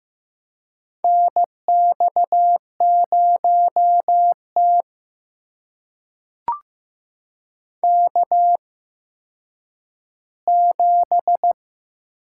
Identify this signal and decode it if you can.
Morse code, heard as a keyed tone, about 15 wpm, spaced as for 9 wpm, 700 Hz. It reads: NX0T K7
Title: USA Call Signs - 4 Characters - 8x Character Spacing - 15wpm